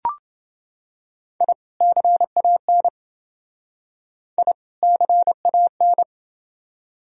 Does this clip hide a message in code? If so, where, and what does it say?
Yes: Morse code from 1.4 to 6.0 s, heard as a keyed tone, about 27 wpm, 700 Hz. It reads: I CAN I CAN